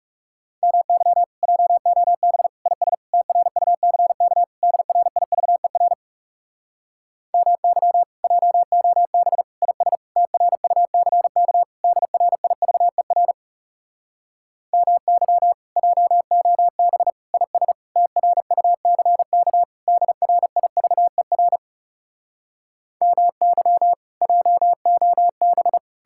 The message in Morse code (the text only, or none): MY JOB IS TRUCK DRIVER MY JOB IS TRUCK DRIVER MY JOB IS TRUCK DRIVER MY JOB